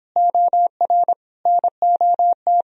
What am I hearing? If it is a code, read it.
Morse code, heard as a keyed tone, about 26 wpm, 700 Hz. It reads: OR NOT